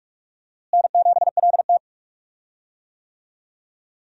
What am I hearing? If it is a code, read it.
Morse code, heard as a keyed tone, about 46 wpm, 700 Hz. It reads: N7LT
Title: USA Call Signs - 4 Characters - Repeated 2x - 45wpm